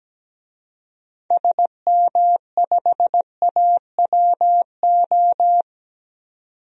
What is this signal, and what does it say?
Morse code, heard as a keyed tone, about 17 wpm, 700 Hz. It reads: SM5AWO
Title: DXCC-100 Call Signs - 6 Characters - 17wpm